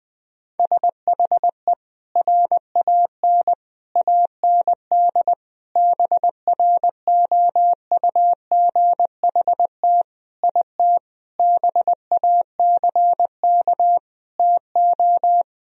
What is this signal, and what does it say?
Morse code, heard as a keyed tone, about 20 wpm, 700 Hz. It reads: SHE RAN AND BROUGHT IT BACK TO
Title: Sentences from Top 1000 Words - Review Part 3 - 20wpm